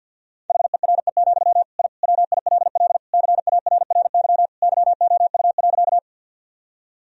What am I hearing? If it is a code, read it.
Morse code, heard as a keyed tone, about 52 wpm, 700 Hz. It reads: HERE, I WILL CARRY YOU.
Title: Sentences from Top 1000 Words - Review Part 1 - 50wpm